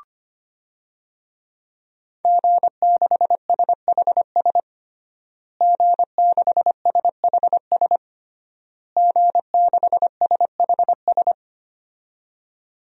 Morse code, heard as a keyed tone, about 25 wpm, 700 Hz. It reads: G6SHS G6SHS G6SHS
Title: DXCC-100 Call Signs - 5 Characters - Repeated 3x - 25wpm